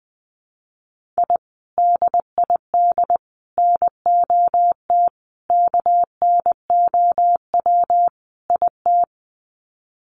Morse code, heard as a keyed tone, about 20 wpm, 700 Hz. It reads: I DID NOT KNOW IT
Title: Sets of 5 Words Encore2 20wpm